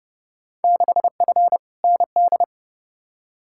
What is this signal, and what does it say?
Morse code, heard as a keyed tone, about 30 wpm, 700 Hz. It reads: 6F ND
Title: Field Day Exchanges Encore - 30wpm